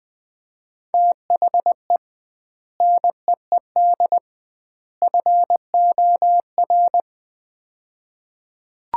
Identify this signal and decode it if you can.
Morse code, heard as a keyed tone, about 21 wpm, 700 Hz. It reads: THE NEED FOR
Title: 2x Word-Spacing - Sets of 3 Words - 20wpm